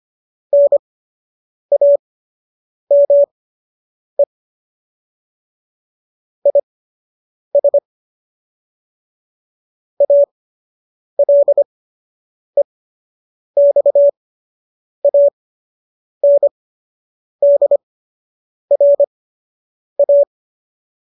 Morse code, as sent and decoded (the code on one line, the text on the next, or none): -. .- -- . / .. ... / .- .-.. -..- .- -. -.. .-. .-
NAME IS ALEXANDRA